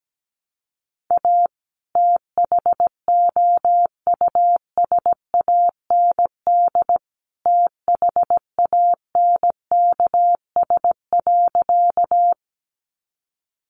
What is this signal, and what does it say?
Morse code, heard as a keyed tone, about 17 wpm, 700 Hz. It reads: A THOUSAND THANKS.
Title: Sentences from Top 600 Words 17wpm